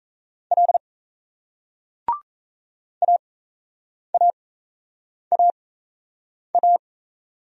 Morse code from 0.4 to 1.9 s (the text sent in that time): L